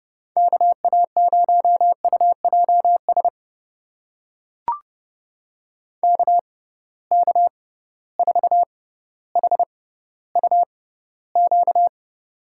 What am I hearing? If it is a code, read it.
Morse code, heard as a keyed tone, about 32 wpm, spaced as for 16 wpm, 700 Hz. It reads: KA0UJS KK4HUQ
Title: USA Call Signs - 6 Characters - 6x Character Spacing - 30wpm